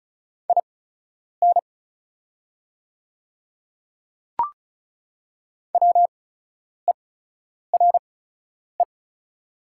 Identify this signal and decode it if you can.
Morse code, heard as a keyed tone, about 34 wpm, spaced as for 10 wpm, 690 Hz. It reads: IN WERE
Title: 8x Character Spacing - Top 200 Words - 35wpm